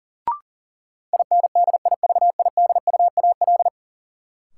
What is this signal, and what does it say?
Morse code, heard as a keyed tone, about 40 wpm, 700 Hz. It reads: INDIVIDUAL
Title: Sentences from Top 1000 Words - Review Part 4 - New Word Drill - 40wpm